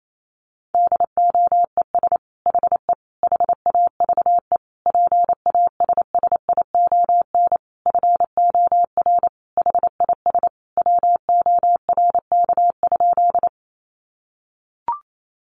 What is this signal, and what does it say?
Morse code, heard as a keyed tone, about 28 wpm, 700 Hz. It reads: DOES HE HAVE PASSION FOR HIS WORK?
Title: Sentences from Top 1300 Words 28wpm